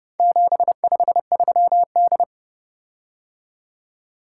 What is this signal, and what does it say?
Morse code, heard as a keyed tone, about 29 wpm, 700 Hz. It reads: 753D